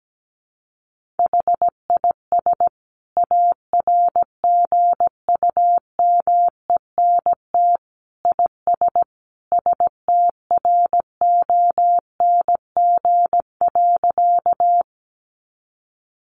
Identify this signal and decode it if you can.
Morse code, heard as a keyed tone, about 17 wpm, 700 Hz. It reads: HIS ARGUMENT IS STRONG.
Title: Sentences from Top 1400 Words 17wpm